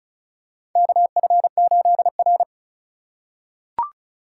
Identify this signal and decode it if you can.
Morse code, heard as a keyed tone, about 36 wpm, 700 Hz. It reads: KF8R
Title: USA Call Signs - 4 Characters - 3x Character Spacing - 35wpm